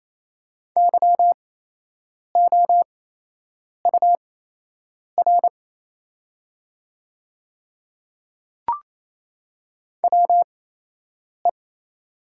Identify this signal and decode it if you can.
Morse code, heard as a keyed tone, about 28 wpm, spaced as for 8 wpm, 690 Hz. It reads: YOUR WE
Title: 8x Character Spacing - Top 100 Words - 28wpm